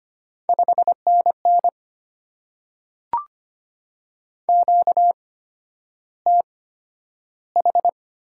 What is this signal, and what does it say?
Morse code, heard as a keyed tone, about 25 wpm, spaced as for 11 wpm, 700 Hz. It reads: 5NN QTH
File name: Top 90 QSO Elements - 8x Character Spacing - 25wpm